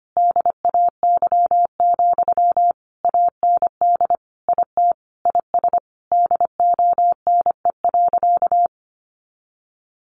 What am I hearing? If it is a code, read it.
Morse code, heard as a keyed tone, about 25 wpm, 700 Hz. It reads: DAY, AND IT IS DONE.
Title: Sentences from Top 1500 Words 25wpm